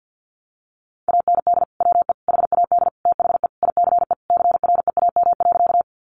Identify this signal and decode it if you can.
Morse code, heard as a keyed tone, about 50 wpm, 700 Hz. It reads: AND WE HAD THE ICE CREAM.
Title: Sentences from Top 1500 Words 50wpm